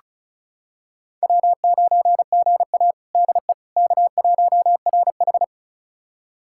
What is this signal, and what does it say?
Morse code, heard as a keyed tone, about 36 wpm, 700 Hz. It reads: W9GA DE K1RH
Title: USA Callsign DE Callsign - 4 Characters - 35wpm